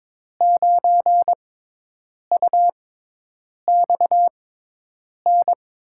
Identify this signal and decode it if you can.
Morse code, heard as a keyed tone, about 22 wpm, spaced as for 8 wpm, 700 Hz. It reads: 9UXN
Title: USA Call Signs - 6x Character Spacing - 22wpm